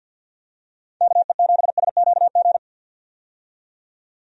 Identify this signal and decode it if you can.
Morse code, heard as a keyed tone, about 47 wpm, 700 Hz. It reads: KE7SQG
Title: USA Call Signs - 6 Characters - 50wpm